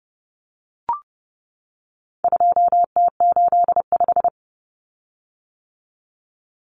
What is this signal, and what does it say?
Morse code, heard as a keyed tone, about 29 wpm, 700 Hz. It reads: 2T85
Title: Common Cut Numbers - 4-Digit 30wpm